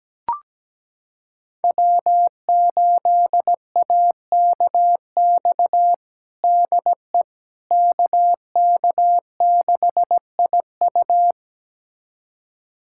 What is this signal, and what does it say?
Morse code, heard as a keyed tone, about 17 wpm, 700 Hz. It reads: W8AKX DE KK6IU
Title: USA Callsign DE Callsign - 5 Characters - 17wpm